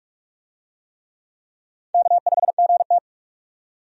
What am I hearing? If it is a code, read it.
Morse code, heard as a keyed tone, about 46 wpm, 700 Hz. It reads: K5GT
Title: USA Call Signs - 4 Characters - 45wpm